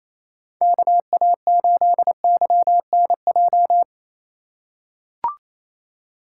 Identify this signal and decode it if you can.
Morse code, heard as a keyed tone, about 28 wpm, 700 Hz. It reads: KA8YNJ
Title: USA Call Signs - 3x Character Spacing - 28wpm